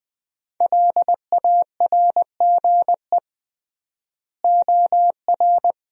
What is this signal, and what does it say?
Morse code, heard as a keyed tone, about 20 wpm, 700 Hz. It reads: LARGE OR
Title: 3x Word-Spacing - Sentences from Top 200 Words - 20wpm